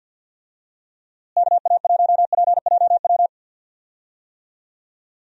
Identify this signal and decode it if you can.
Morse code, heard as a keyed tone, about 46 wpm, 700 Hz. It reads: KA1PJW
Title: USA Call Signs - 6 Characters - Repeated 3x - 50wpm